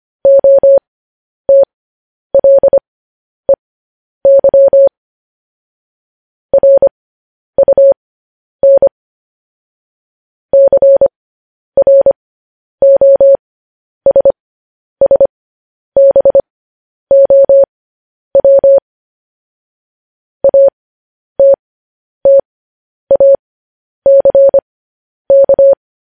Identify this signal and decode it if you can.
Morse code, heard as a keyed tone, about 25 wpm, spaced as for 10 wpm, 550 Hz. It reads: OTLEY RUN CROSSBOW ATTACK
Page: news headlines in morse code at 10 wpm